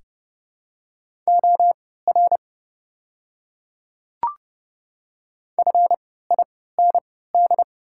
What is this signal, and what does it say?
Morse code, heard as a keyed tone, about 32 wpm, spaced as for 17 wpm, 700 Hz. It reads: OR FIND
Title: 3x Character Spacing - Top 500 Words - 30wpm